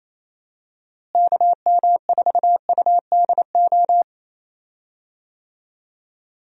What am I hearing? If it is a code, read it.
Morse code, heard as a keyed tone, about 28 wpm, 700 Hz. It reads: KM4UDO
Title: USA Call Signs - 6 Characters - Repeated 2x - 28wpm